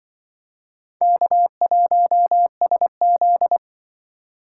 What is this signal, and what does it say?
Morse code, heard as a keyed tone, about 24 wpm, 700 Hz. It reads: K1SZ
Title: Speed Racing - USA Call Signs 20wpm